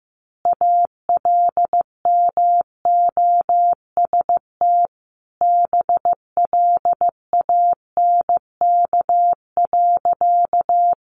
Morse code, heard as a keyed tone, about 15 wpm, 700 Hz. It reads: ALMOST BLANK.
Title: Sentences from Top 1500 Words 15wpm